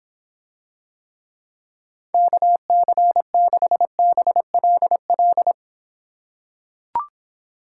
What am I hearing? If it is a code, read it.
Morse code, heard as a keyed tone, about 26 wpm, 700 Hz. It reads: KC6BLL